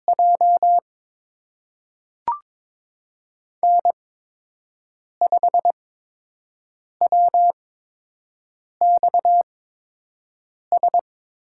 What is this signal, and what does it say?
Morse code, heard as a keyed tone, about 23 wpm, spaced as for 6 wpm, 700 Hz. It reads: J N5WXS
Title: USA Call Signs - 5 Characters - 8x Character Spacing - 22wpm